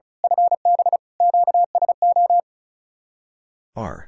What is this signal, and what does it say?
Morse code, heard as a keyed tone, about 38 wpm, 700 Hz. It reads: FB QSO
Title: Speed-Racing Common QSO Phrases Encore - 35wpm